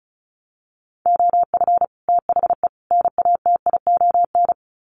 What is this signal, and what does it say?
Morse code, heard as a keyed tone, about 34 wpm, 700 Hz. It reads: OF THE NATION
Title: Sets of 3 Words 35wpm